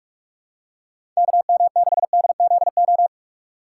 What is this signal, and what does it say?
Morse code, heard as a keyed tone, about 45 wpm, 700 Hz. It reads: KM6DZO
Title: USA Call Signs - 2x Character Spacing - 45wpm